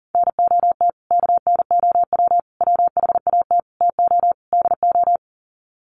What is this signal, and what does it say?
Morse code, heard as a keyed tone, about 40 wpm, 700 Hz. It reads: NOT KNOW WHAT TO DO
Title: Sets of 5 Words 40wpm